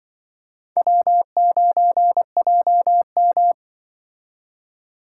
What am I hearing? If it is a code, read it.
Morse code, heard as a keyed tone, about 24 wpm, 700 Hz. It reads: W9JM
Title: Speed Racing - USA 1x2 Call Signs 20wpm